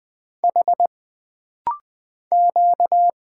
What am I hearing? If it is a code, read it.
Morse code, heard as a keyed tone, about 20 wpm, 700 Hz. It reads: H Q
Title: Single Letters - Rapid-Fire 20wpm